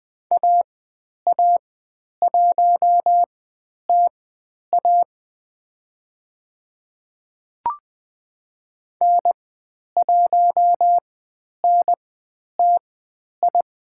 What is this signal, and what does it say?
Morse code, heard as a keyed tone, about 20 wpm, spaced as for 10 wpm, 700 Hz. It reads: AA1TA N1NTI
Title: 109 Lesson 18g - Letter m - Call Signs 20wpm